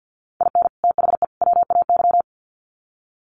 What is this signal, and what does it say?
Morse code, heard as a keyed tone, about 48 wpm, 700 Hz. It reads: IN THE WAY
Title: Sets of 3 Words 50wpm